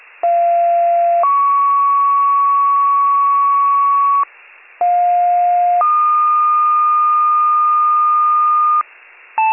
Two-tone page, heard as a one-second tone, then a three-second tone, about 690 then 1120 Hz, about 710 then 1220 Hz. It is heard from 0.2 to 8.8 s.